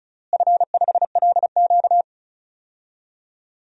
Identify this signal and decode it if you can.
Morse code, heard as a keyed tone, about 34 wpm, 700 Hz. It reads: F5LQ
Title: DXCC-100 Call Signs - 4 Characters - Repeated 3x - 35wpm